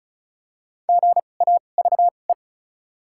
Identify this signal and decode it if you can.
Morse code, heard as a keyed tone, about 36 wpm, spaced as for 25 wpm, 700 Hz. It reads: GAVE
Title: 2x Character Spacing - Top 500 Words - 35wpm